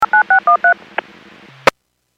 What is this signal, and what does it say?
DTMF, heard as telephone touch tones, tones of about 110 ms. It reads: 9613